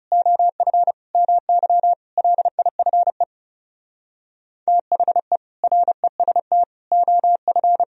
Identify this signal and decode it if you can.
Morse code, heard as a keyed tone, about 37 wpm, 700 Hz. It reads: OF MY LIFE THE REST OF
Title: Speed-Racing - Sets of 5 Words 30wpm